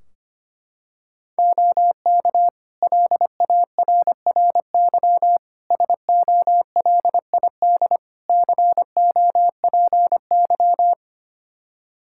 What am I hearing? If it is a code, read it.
Morse code, heard as a keyed tone, about 25 wpm, 700 Hz. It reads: OK LARRY SOLID COPY
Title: Practice QSO - Exchange 3 - More Info - 25wpm